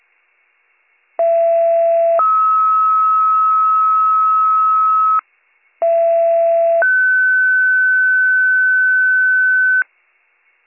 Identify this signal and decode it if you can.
Two-tone page, heard as a one-second tone, then a three-second tone, about 670 then 1290 Hz, about 670 then 1500 Hz. It is heard from 1.2 to 9.8 s.